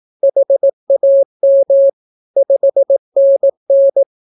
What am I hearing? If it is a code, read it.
Morse code, heard as a keyed tone, about 18 wpm, 550 Hz. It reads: HAM 5NN
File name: naukapodstaw / skroty / skroty